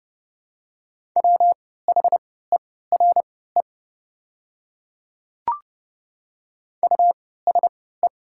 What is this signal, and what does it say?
Morse code, heard as a keyed tone, about 32 wpm, spaced as for 17 wpm, 700 Hz. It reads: WHERE USE
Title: 3x Character Spacing - Top 300 Words - 30wpm